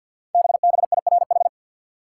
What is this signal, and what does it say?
Morse code, heard as a keyed tone, about 49 wpm, 700 Hz. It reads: BBIRH
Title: Code Groups - 5 Letters - 50wpm